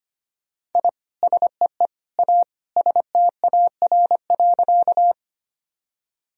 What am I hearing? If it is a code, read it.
Morse code, heard as a keyed tone, about 25 wpm, 700 Hz. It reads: I SEE A STAR.
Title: Sentences from Top 1000 Words - Review Part 2 - 25wpm